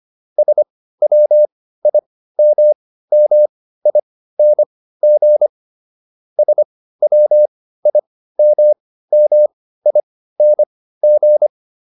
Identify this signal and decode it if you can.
Morse code, heard as a keyed tone, about 25 wpm, spaced as for 15 wpm, 600 Hz. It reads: SWIMMING SWIMMING